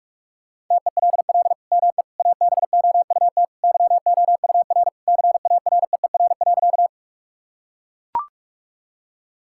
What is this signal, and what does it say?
Morse code, heard as a keyed tone, about 44 wpm, 700 Hz. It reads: TELL ME ABOUT YOUR CAREER.
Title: Sentences from Top 1000 Words - Review Part 6 - 45wpm